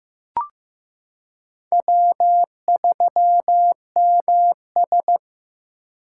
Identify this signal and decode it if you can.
Morse code, heard as a keyed tone, about 15 wpm, 700 Hz. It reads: W3MS
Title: K1USN SST QSO Practice - 15wpm